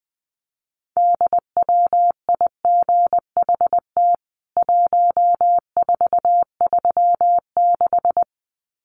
Morse code, heard as a keyed tone, about 20 wpm, 700 Hz. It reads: DWIGHT 1436
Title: CWT Exchange 20wpm